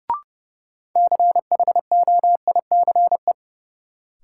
Morse code, heard as a keyed tone, about 30 wpm, 700 Hz. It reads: CHOICE
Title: Sentences from Top 900 Words - New Word Drill - 30wpm